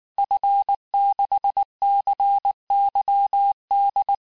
Morse code, heard as a keyed tone, about 19 wpm, 800 Hz. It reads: F6CYD
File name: cours-du-lundi files / cours-1589-texte-FIN-de-QSO-vitesse-18mpm